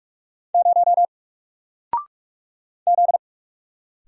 Morse code, heard as a keyed tone, about 44 wpm, 700 Hz. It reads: 0 Z
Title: Single Letter-Number Alt 45wpm